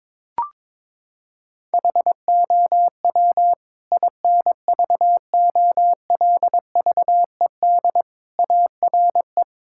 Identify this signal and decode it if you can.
Morse code, heard as a keyed tone, about 22 wpm, 700 Hz. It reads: HOW INVOLVED ARE
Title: Sentences from Top 1000 Words - Review Part 5 - 22wpm